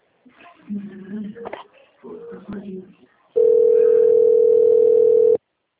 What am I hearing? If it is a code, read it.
Ringback tone (North America), 440 + 480 Hz, with speech.
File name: rg-1600-unknown- - - .85006